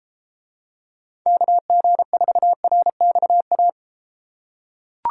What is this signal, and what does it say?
Morse code, heard as a keyed tone, about 33 wpm, 700 Hz. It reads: KG4RXA